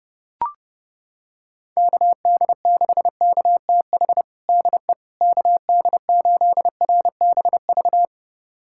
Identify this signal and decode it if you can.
Morse code, heard as a keyed tone, about 30 wpm, 700 Hz. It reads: KD6KTH DE KD8RBV
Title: USA Callsign DE Callsign - 6 Characters - 30wpm